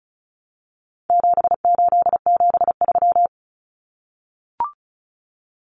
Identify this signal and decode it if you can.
Morse code, heard as a keyed tone, about 34 wpm, 700 Hz. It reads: 7873
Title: Common Cut Numbers - 4-Digit 35wpm